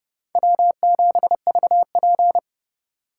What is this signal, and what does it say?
Morse code, heard as a keyed tone, about 30 wpm, 700 Hz. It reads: W7VP
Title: K1USN SST QSO Practice - 30wpm